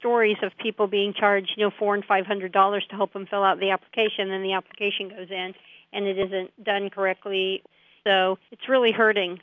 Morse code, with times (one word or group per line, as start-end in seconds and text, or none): none